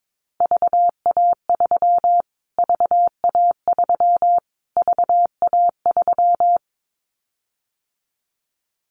Morse code, heard as a keyed tone, about 22 wpm, 700 Hz. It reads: VA3 VA3 VA3
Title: Callsign Prefixes for Top 25 DXCC - Repeated 3x - 22wpm